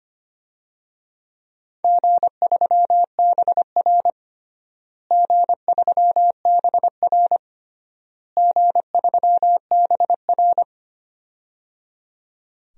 Morse code, heard as a keyed tone, about 25 wpm, 700 Hz. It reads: G3BR G3BR G3BR
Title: DXCC-100 Call Signs - 4 Characters - Repeated 3x - 25wpm